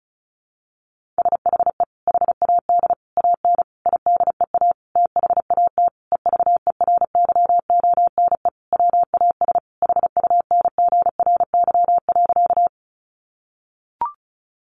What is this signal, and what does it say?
Morse code, heard as a keyed tone, about 34 wpm, 700 Hz. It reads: SHE HAD AN IDEA THAT EVERYONE WAS HUNGRY.
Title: Sentences from Top 1400 Words 35wpm